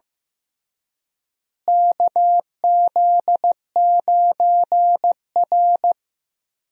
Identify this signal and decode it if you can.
Morse code, heard as a keyed tone, about 15 wpm, 700 Hz. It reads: KZ9R
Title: USA Call Signs - 4 Characters - Repeated 3x - 15wpm